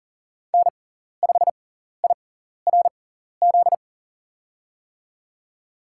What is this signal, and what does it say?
Morse code, heard as a keyed tone, about 40 wpm, spaced as for 14 wpm, 700 Hz. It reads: N5IRZ
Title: USA Call Signs - 6x Character Spacing - 40wpm